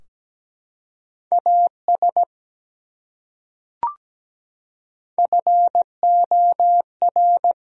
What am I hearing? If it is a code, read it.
Morse code, heard as a keyed tone, about 17 wpm, 700 Hz. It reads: AS FOR
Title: Top 100 Words Repeated 17wpm